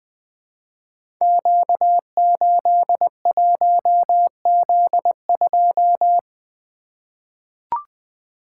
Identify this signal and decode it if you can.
Morse code, heard as a keyed tone, about 20 wpm, 700 Hz. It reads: Q81Z2
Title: Code Groups - 5 Letter:Numbers - 20wpm